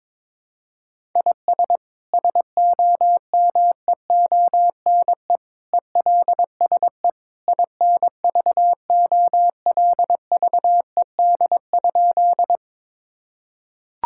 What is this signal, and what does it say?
Morse code, heard as a keyed tone, about 22 wpm, 700 Hz. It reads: IS SOMEONE ELSE INVOLVED?
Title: Sentences from Top 1000 Words - Review Part 5 - 22wpm